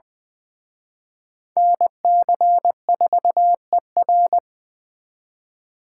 Morse code, heard as a keyed tone, about 24 wpm, 700 Hz. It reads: NC4ER
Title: Speed Racing - USA Call Signs 20wpm